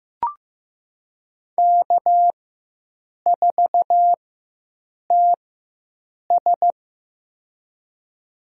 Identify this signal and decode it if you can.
Morse code, heard as a keyed tone, about 15 wpm, spaced as for 7 wpm, 700 Hz. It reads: K4TS
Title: USA Call Signs - 4 Characters - 4x Character Spacing - 15wpm